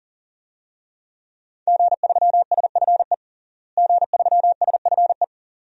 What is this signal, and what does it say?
Morse code, heard as a keyed tone, about 40 wpm, 700 Hz. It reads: G3SFE G3SFE
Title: DXCC-100 Call Signs - 5 Characters - Repeated 2x - 40wpm